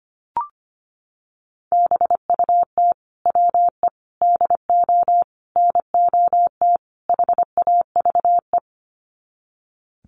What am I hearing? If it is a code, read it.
Morse code, heard as a keyed tone, about 25 wpm, 700 Hz. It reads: BUT WE DO NOT HAVE
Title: Sets of 5 Words Encore2 25wpm